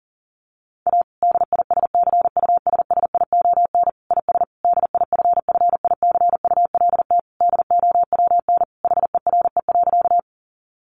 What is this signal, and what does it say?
Morse code, heard as a keyed tone, about 40 wpm, 700 Hz. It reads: A DISCUSSION IS DIFFICULT DOWN HERE.
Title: Sentences from Top 1100 Words 40wpm